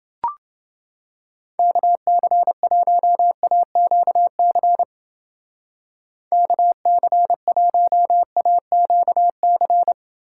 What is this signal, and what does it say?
Morse code, heard as a keyed tone, about 29 wpm, 700 Hz. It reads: KC1AQC KC1AQC